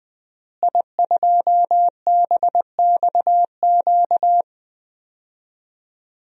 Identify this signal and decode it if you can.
Morse code, heard as a keyed tone, about 20 wpm, 700 Hz. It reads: I2BXQ
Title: DXCC-100 Call Signs - 5 Characters - Repeated 3x - 20wpm